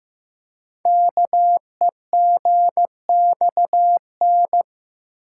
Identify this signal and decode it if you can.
Morse code, heard as a keyed tone, about 15 wpm, 690 Hz. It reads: KEGXN
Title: Code Groups - 5 Letters - 15wpm